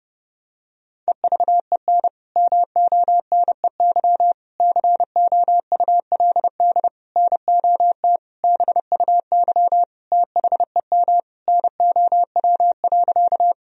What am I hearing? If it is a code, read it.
Morse code, heard as a keyed tone, about 30 wpm, 700 Hz. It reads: EVEN MONEY COULD NOT BUY THEM NOW.
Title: Sentences from Top 1000 Words - Review Part 4 - 30wpm